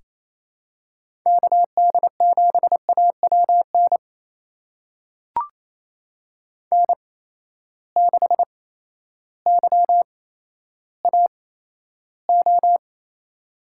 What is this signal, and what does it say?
Morse code, heard as a keyed tone, about 28 wpm, spaced as for 13 wpm, 700 Hz. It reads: KD7AWN N6YAO
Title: USA Call Signs - 8x Character Spacing - 28wpm